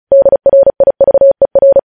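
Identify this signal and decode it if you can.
Morse code, heard as a keyed tone, about 36 wpm, 550 Hz. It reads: DRIVER
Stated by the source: Morse practice sending news headlines at 35 wpm.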